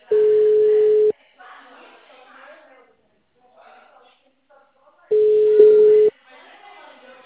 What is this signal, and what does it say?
Ringback tone (Europe, 425 Hz), with speech.